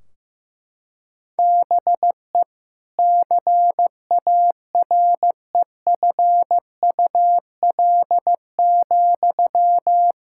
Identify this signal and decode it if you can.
Morse code, heard as a keyed tone, about 16 wpm, 700 Hz. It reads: BE CAREFUL,
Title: Sentences from Top 1600 Words 15wpm